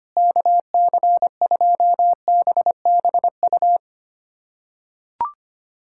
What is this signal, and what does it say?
Morse code, heard as a keyed tone, about 25 wpm, 700 Hz. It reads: KC2BBU
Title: K1USN SST QSO Practice - 25wpm